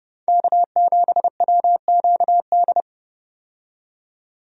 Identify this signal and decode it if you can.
Morse code, heard as a keyed tone, about 30 wpm, 700 Hz. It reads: K7WQD
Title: USA Call Signs - 5 Characters - 30wpm